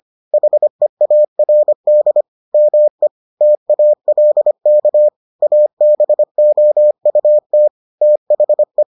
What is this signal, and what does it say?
Morse code, heard as a keyed tone, about 25 wpm, 600 Hz. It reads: HEARD ME TALK ABOUT THE